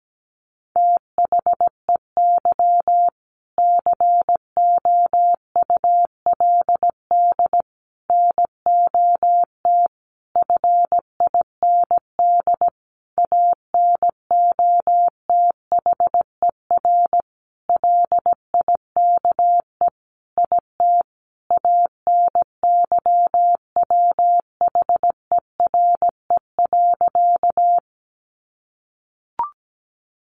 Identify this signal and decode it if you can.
Morse code, heard as a keyed tone, about 17 wpm, 700 Hz. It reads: THEY COULD NOT FIND ANOTHER LIKE IT ANYWHERE.
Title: Sentences from Top 1400 Words 17wpm